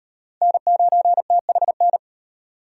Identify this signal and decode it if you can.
Morse code, heard as a keyed tone, about 38 wpm, 700 Hz. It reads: N9THN